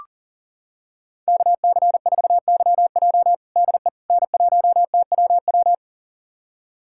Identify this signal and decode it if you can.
Morse code, heard as a keyed tone, about 37 wpm, 700 Hz. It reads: KC4YJ DE N1TWW
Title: USA Callsign DE Callsign - 5 Characters - 40wpm